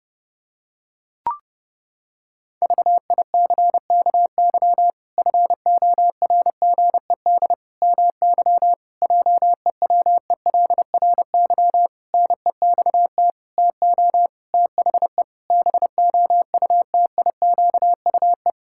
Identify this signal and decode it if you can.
Morse code, heard as a keyed tone, about 30 wpm, 700 Hz. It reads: VICKY FORGED MY JEWELRY NEXT TO THE BOUTIQUE
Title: Pangram Challenge v2 - 30wpm